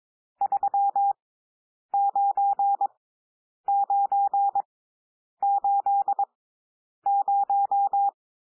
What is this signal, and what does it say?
Morse code, heard as a keyed tone, about 22 wpm, spaced as for 9 wpm, 800 Hz. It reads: 39980
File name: cwmp3 / N1 SOUND NUMBERS 22 wpm